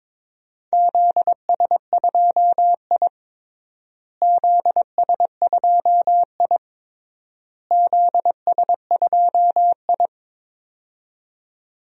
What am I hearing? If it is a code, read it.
Morse code, heard as a keyed tone, about 22 wpm, 700 Hz. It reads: ZS2I ZS2I ZS2I
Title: DXCC-100 Call Signs - 4 Characters - Repeated 3x - 22wpm